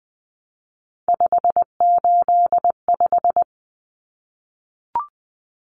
Morse code, heard as a keyed tone, about 20 wpm, 700 Hz. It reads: 585
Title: Numbers - 3-Digit 20wpm